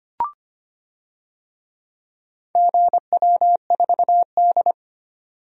Morse code, heard as a keyed tone, about 25 wpm, 700 Hz. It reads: GW4D
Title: DXCC-100 Call Signs - 4 Characters - 25wpm